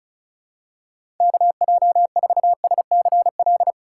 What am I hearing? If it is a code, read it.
Morse code, heard as a keyed tone, about 36 wpm, 700 Hz. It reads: KJ4SCL
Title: USA Call Signs - 6 Characters - 4x Character Spacing - 35wpm